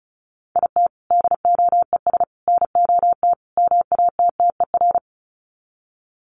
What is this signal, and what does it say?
Morse code, heard as a keyed tone, about 34 wpm, 700 Hz. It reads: IT DOES NOT MATTER
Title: Sets of 4 Words 35wpm